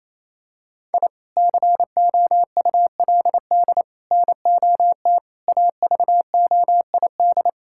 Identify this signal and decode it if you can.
Morse code, heard as a keyed tone, about 27 wpm, 700 Hz. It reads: I COULD NOT AVOID
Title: Sentences from Top 1000 Words - Review Part 6 - 28wpm